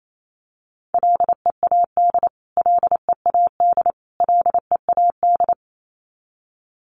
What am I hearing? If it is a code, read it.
Morse code, heard as a keyed tone, about 28 wpm, 700 Hz. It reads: LEAD LEAD LEAD
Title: Top 500 Words Repeated 28wpm